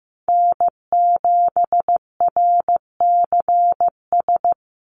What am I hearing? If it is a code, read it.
Morse code, heard as a keyed tone, about 15 wpm, 700 Hz. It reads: N7RCS
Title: Call Signs Repeated 3x - 15wpm